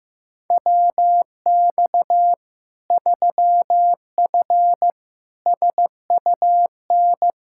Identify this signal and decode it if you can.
Morse code, heard as a keyed tone, about 16 wpm, 700 Hz. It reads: WX 3F SUN
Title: Practice QSO - Exchange 2 - Info - 15wpm